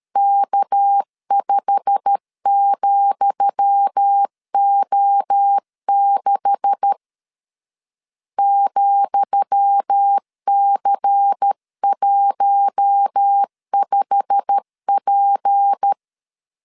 Morse code, heard as a keyed tone, about 13 wpm, 800 Hz. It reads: K5,O6 ,C15P
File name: LEZIONI / 10 SIGNS / 10D COMA 12wpm 2544